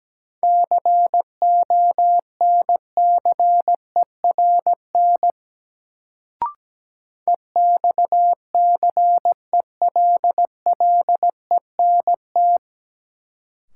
Morse code, heard as a keyed tone, about 17 wpm, 700 Hz. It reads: CONCERN EXCELLENT
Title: Sentences from Top 1000 Words - New Word Drill - 17wpm